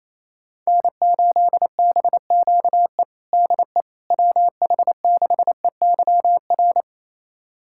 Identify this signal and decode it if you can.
Morse code, heard as a keyed tone, about 28 wpm, 700 Hz. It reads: N8BQE DE WH6EYR
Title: USA Callsign DE Callsign 28wpm